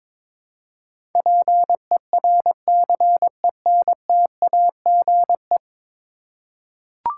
Morse code, heard as a keyed tone, about 22 wpm, 700 Hz. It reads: PERCENTAGE
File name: practice / 22wpm / 10 Letter Words - First 3 Letters Given - 22wpm